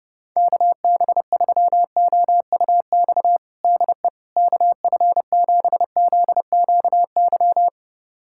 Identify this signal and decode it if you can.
Morse code, heard as a keyed tone, about 32 wpm, 700 Hz. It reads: KB3OUX DE KF7ZQY
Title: USA Callsign DE Callsign - 6 Characters - 30wpm